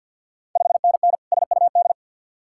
Morse code, heard as a keyed tone, about 46 wpm, 700 Hz. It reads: 5NN SUD
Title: Ontario QSO Party Exchange 50wpm